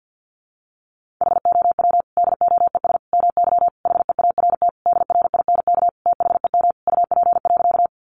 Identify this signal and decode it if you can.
Morse code, heard as a keyed tone, about 48 wpm, 700 Hz. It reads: HOW DOES MY HEART DRINK THEM UP.
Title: Sentences from Top 1500 Words 50wpm